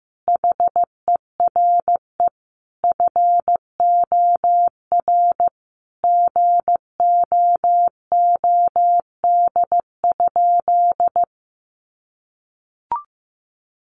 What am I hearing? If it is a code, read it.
Morse code, heard as a keyed tone, about 15 wpm, 700 Hz. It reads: HERE FOR GOOD?
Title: Sentences from Top 200 Words 15wpm